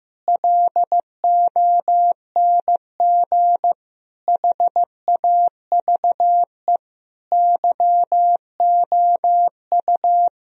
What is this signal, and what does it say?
Morse code, heard as a keyed tone, about 15 wpm, 700 Hz. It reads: LONG HAVE YOU
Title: Sentences from Top 1000 Words - Review Part 3 - 15wpm